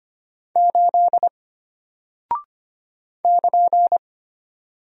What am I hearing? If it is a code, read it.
Morse code, heard as a keyed tone, about 25 wpm, 700 Hz. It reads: 8 <KN>
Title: Single Character (Letter:Number:Punctuation:Prosign) Alt 25wpm